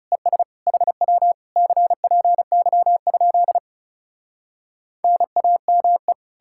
Morse code, heard as a keyed tone, about 40 wpm, 700 Hz. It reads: ES HW CPY? NAME